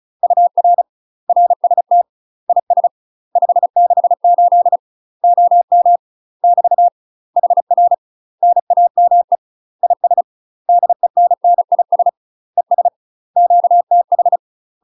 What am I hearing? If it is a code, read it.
Morse code, heard as a keyed tone, about 36 wpm, 700 Hz. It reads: UR RST IS 568 OM <BT> HR NAME IS DENNIS ES QTH
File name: UK MASTER QSOs TRAINING / 35WPM / UK MASTER QSOs 0007